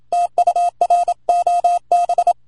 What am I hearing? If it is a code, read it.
Morse code, heard as a keyed tone, about 27 wpm, 700 Hz. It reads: TU R OB